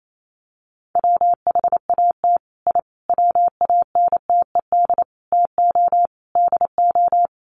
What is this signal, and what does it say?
Morse code, heard as a keyed tone, about 28 wpm, 700 Hz. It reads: WHAT I WANTED TO DO
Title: Sets of 5 Words Encore2 28wpm